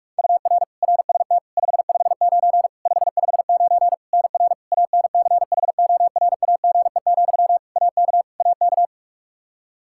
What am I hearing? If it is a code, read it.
Morse code, heard as a keyed tone, about 46 wpm, 700 Hz. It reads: UR RST 559 559 NR ANCHORAGE, AK AK
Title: Practice QSO - Exchange 1 - Beginning - 45wpm